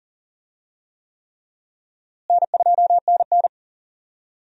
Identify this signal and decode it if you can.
Morse code, heard as a keyed tone, about 40 wpm, 700 Hz. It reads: N2NN
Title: USA Call Signs - 4 Characters - Repeated 3x - 40wpm